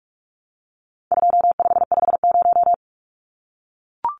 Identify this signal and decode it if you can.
Morse code, heard as a keyed tone, about 44 wpm, 700 Hz. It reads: 2550
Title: Numbers - 4-Digit 45wpm